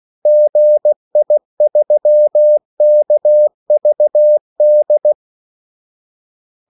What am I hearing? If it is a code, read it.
Morse code, heard as a keyed tone, about 16 wpm, 600 Hz. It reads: GI3KVD
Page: Kaldesignaler 16 wpm